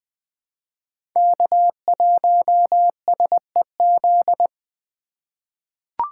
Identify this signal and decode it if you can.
Morse code, heard as a keyed tone, about 20 wpm, 700 Hz. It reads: K1SEZ
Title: USA Call Signs - 5 Characters - 6x Character Spacing - 20wpm